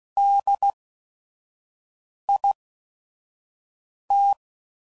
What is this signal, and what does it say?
Morse code, heard as a keyed tone, about 16 wpm, spaced as for 5 wpm, 800 Hz. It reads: DIT
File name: Prefixes / By First Letter / Ds